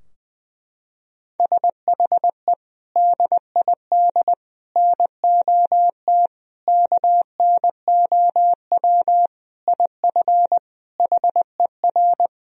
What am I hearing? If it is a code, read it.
Morse code, heard as a keyed tone, about 21 wpm, 700 Hz. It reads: SHE DID NOT KNOW IF HER
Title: Sentences from Top 1000 Words - Review Part 4 - 20wpm